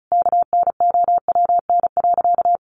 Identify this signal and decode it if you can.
Morse code, heard as a keyed tone, about 34 wpm, 700 Hz. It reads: KNOWN.
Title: Sentences from Top 1300 Words 35wpm